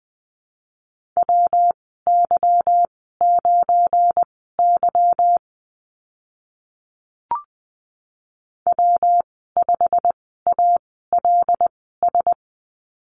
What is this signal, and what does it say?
Morse code, heard as a keyed tone, about 20 wpm, spaced as for 15 wpm, 700 Hz. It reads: WY9Y W5ALS
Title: 219 Lesson 35g - Number 0 - Call Signs 20wpm